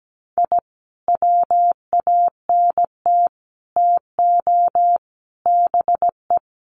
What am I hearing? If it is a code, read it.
Morse code, heard as a keyed tone, about 17 wpm, 700 Hz. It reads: I WANT TO BE
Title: Sets of 5 Words Encore 17wpm